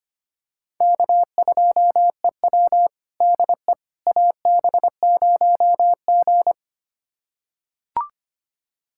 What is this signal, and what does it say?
Morse code, heard as a keyed tone, about 25 wpm, 700 Hz. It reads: K2EW DE AB0G
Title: USA Callsign DE Callsign - 4 Characters - 25wpm